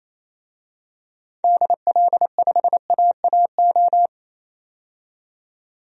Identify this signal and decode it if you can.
Morse code, heard as a keyed tone, about 28 wpm, 700 Hz. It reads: DL5AAO
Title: DXCC-100 Call Signs - 6 Characters - 28wpm